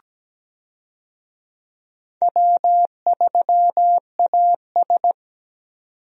Morse code, heard as a keyed tone, about 17 wpm, 700 Hz. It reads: W3AS
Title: USA Call Signs - 4 Characters - 17wpm